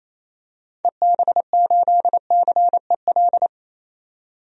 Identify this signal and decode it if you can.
Morse code, heard as a keyed tone, about 28 wpm, 700 Hz. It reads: EB8CEL
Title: DXCC-100 Call Signs - 6 Characters - 28wpm